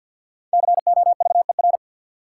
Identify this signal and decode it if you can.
Morse code, heard as a keyed tone, about 48 wpm, 700 Hz. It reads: COVER
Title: Sentences from Top 500 Words - New Word Drill - 50wpm